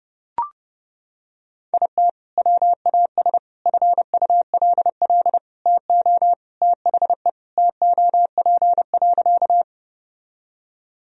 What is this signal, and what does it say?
Morse code, heard as a keyed tone, about 30 wpm, 700 Hz. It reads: IT WAS FULL TO THE TOP.
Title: Sentences from Top 1000 Words - Review Part 3 - 30wpm